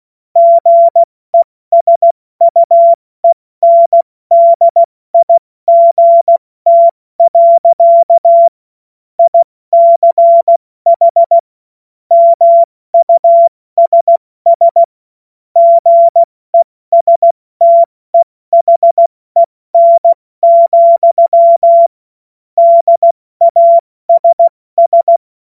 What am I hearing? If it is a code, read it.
Morse code, heard as a keyed tone, about 16 wpm, 680 Hz. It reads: GESUENDIGT. ICH MUSS GESTEHEN, DASS
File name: Morsekurs / Witze / Witz 06. September 2022 70 F80